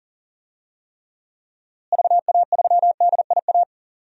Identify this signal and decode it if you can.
Morse code, heard as a keyed tone, about 40 wpm, 700 Hz. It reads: VA3DIA
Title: DXCC-100 Call Signs - 6 Characters - Repeated 2x - 40wpm